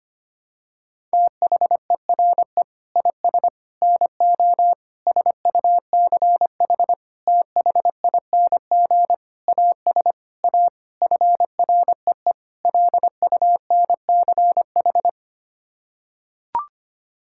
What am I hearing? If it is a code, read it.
Morse code, heard as a keyed tone, about 25 wpm, 700 Hz. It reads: THERE IS NO SUCH THING AS A FREE LUNCH
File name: Sets of 9 Words - 25wpm